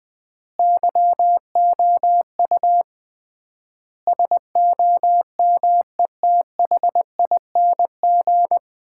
Morse code, heard as a keyed tone, about 21 wpm, 700 Hz. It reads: YOU SOMETHING
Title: 3x Word-Spacing - Sets of 5 Words - 20wpm